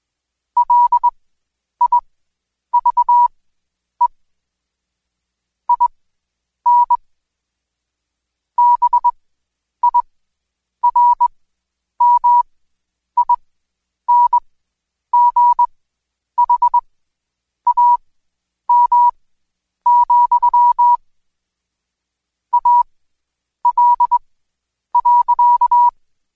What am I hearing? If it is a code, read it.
Morse code, heard as a keyed tone, about 21 wpm, spaced as for 9 wpm, 970 Hz. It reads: LIVE IN BIRMINGHAM, AL.
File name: Code Quick / AudioDownld2 / Seg116-Day21-DayDrill 21